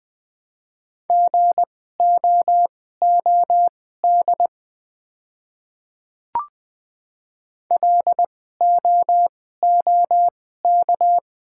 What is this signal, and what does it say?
Morse code, heard as a keyed tone, about 20 wpm, spaced as for 14 wpm, 700 Hz. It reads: GOOD LOOK
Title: 2x Character Spacing - Top 200 Words - 20wpm